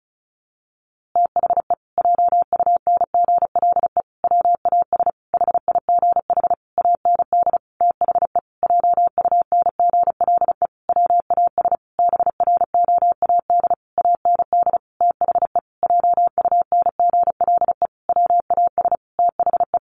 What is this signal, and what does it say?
Morse code, heard as a keyed tone, about 34 wpm, 700 Hz. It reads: THE JUNGLE WAS HIGH AND THE JUNGLE WAS BROAD AND THE JUNGLE WAS THE
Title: A Sound Of Thunder 35wpm